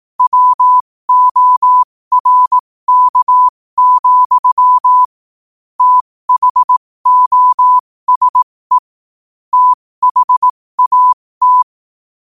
Morse code, heard as a keyed tone, about 18 wpm, spaced as for 15 wpm, 1000 Hz. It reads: WORK, THOSE THAT
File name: qotdfiles / 15wpm